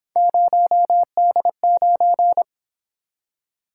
Morse code, heard as a keyed tone, about 26 wpm, 700 Hz. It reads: 0D9
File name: Speed-Racing - Three Letter-Number Combinations - 22wpm